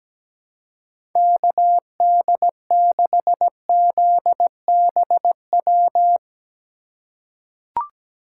Morse code, heard as a keyed tone, about 17 wpm, 700 Hz. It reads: KD6ZBW
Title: USA Call Signs - 6 Characters - 2x Character Spacing - 17wpm